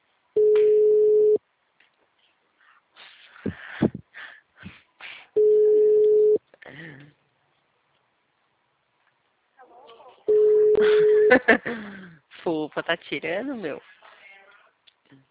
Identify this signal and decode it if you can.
Ringback tone (Europe, 425 Hz), with speech.